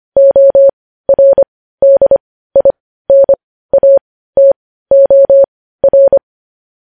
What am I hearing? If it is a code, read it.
Morse code, heard as a keyed tone, about 25 wpm, spaced as for 15 wpm, 550 Hz. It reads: ORDINATOR
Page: news headlines in morse code at 15 wpm